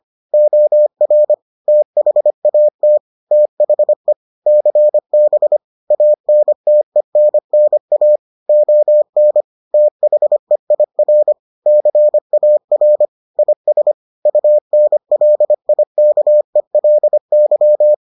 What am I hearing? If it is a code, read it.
Morse code, heard as a keyed tone, about 25 wpm, 600 Hz. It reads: OR THAT THE CB ANTENNA ON THEIR CAR IS UNLIKELY